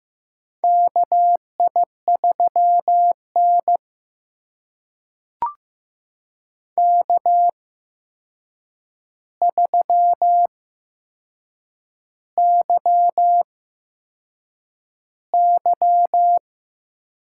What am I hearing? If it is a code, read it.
Morse code, heard as a keyed tone, about 15 wpm, spaced as for 6 wpm, 700 Hz. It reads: KI3N K3YY
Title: USA Call Signs - 4 Characters - 8x Character Spacing - 15wpm